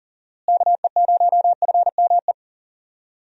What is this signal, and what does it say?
Morse code, heard as a keyed tone, about 40 wpm, 700 Hz. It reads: KE0FME
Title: USA Call Signs - 6 Characters - Repeated 3x - 40wpm